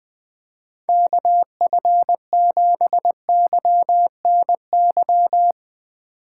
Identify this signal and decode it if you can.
Morse code, heard as a keyed tone, about 20 wpm, 700 Hz. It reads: KF7YNY